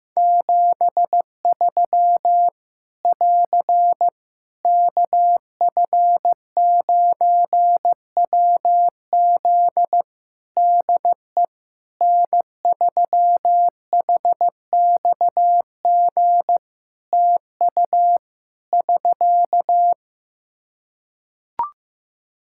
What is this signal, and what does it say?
Morse code, heard as a keyed tone, about 15 wpm, 700 Hz. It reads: 73 <AR> KF9WZ DE N3HXG TU <SK>
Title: Practice QSO - Exchange 4 - Ending - 15wpm